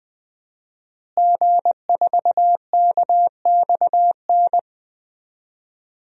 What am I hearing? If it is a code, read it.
Morse code, heard as a keyed tone, about 20 wpm, 700 Hz. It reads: G4KXN
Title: DXCC-100 Call Signs - 5 Characters - Repeated 2x - 20wpm